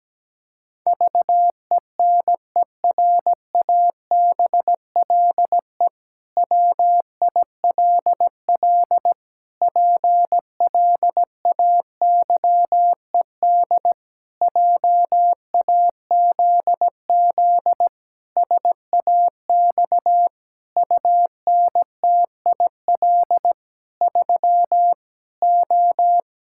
Morse code, heard as a keyed tone, about 17 wpm, 700 Hz. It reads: VENERABLE WILL PLAYED JAZZ SAX UNTIL 3 O
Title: Pangram Challenge v2 - 17wpm